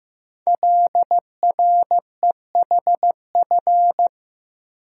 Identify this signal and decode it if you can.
Morse code, heard as a keyed tone, about 15 wpm, 700 Hz. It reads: LREHF
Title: Code Groups - 5 Letters - 15wpm